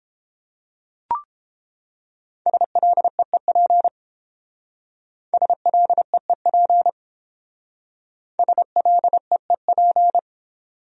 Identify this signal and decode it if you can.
Morse code, heard as a keyed tone, about 28 wpm, 700 Hz. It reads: SLEEP SLEEP SLEEP